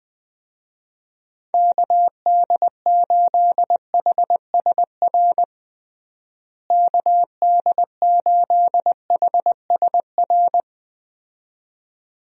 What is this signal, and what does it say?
Morse code, heard as a keyed tone, about 20 wpm, 700 Hz. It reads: KD8HSR KD8HSR
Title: USA Call Signs - 6 Characters - Repeated 2x - 20wpm